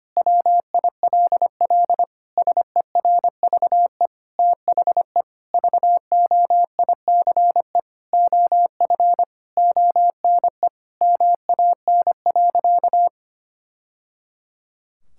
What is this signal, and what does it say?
Morse code, heard as a keyed tone, about 25 wpm, 700 Hz. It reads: WILL SERVE THE VOICE OF ONE MAN.
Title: Sentences from Top 1000 Words - Review Part 6 - 25wpm